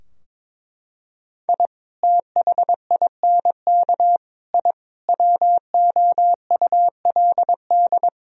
Morse code, heard as a keyed tone, about 22 wpm, 700 Hz. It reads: I THINK I WOULD
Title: Sentences from Top 1000 Words - Review Part 4 - 22wpm